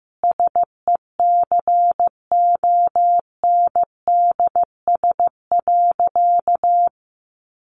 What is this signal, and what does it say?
Morse code, heard as a keyed tone, about 15 wpm, 700 Hz. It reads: SECONDS.
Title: A Sound Of Thunder 15wpm